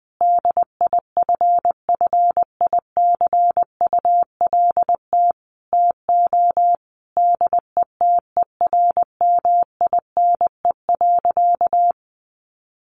Morse code, heard as a keyed tone, about 20 wpm, 700 Hz. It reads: DIFFICULT TO DETERMINE.
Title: Sentences from Top 1500 Words 20wpm